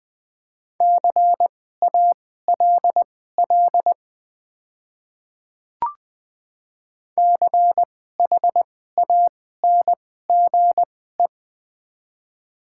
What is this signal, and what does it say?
Morse code, heard as a keyed tone, about 20 wpm, spaced as for 14 wpm, 700 Hz. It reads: CALL CHANGE
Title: 2x Character Spacing - Top 200 Words - 20wpm